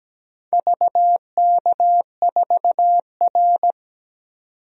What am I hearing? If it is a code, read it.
Morse code, heard as a keyed tone, about 17 wpm, 700 Hz. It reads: VK4R
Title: DXCC-100 Call Signs - 4 Characters - 17wpm